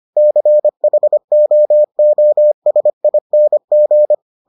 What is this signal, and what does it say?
Morse code, heard as a keyed tone, about 25 wpm, 600 Hz. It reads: CHOOSING